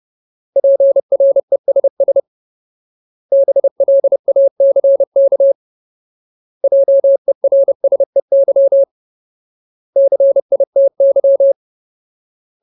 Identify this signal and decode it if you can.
Morse code, heard as a keyed tone, about 30 wpm, 550 Hz. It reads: PRESS BLACK JERSEY CITY
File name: Ultimate Word List / 30WPM Ultimate